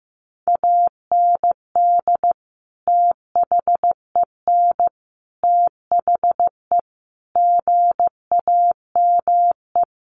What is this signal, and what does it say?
Morse code, heard as a keyed tone, about 14 wpm, 700 Hz. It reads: AND THEN THE GAME
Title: Sentences from Top 500 Words 15wpm